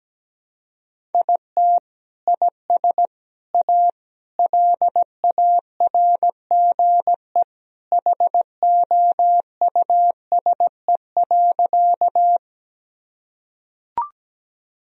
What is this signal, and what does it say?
Morse code, heard as a keyed tone, about 17 wpm, 700 Hz. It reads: IT IS A LARGE HOUSE.
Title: Sentences from Top 1000 Words - Review Part 1 - 17wpm